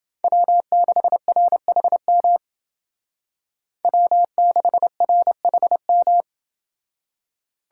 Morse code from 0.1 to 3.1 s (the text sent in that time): W6RHM